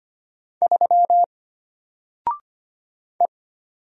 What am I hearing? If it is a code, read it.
Morse code, heard as a keyed tone, about 25 wpm, 700 Hz. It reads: 3 E
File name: Single-Letter-Number-25wpm